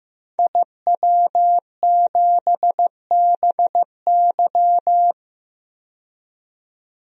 Morse code, heard as a keyed tone, about 15 wpm, 700 Hz. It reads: IW7BY
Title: DXCC-100 Call Signs - 5 Characters - 15wpm